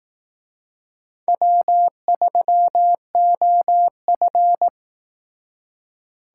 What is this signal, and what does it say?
Morse code, heard as a keyed tone, about 18 wpm, 700 Hz. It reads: W3OF
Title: Speed Racing - USA 1x2 Call Signs 15wpm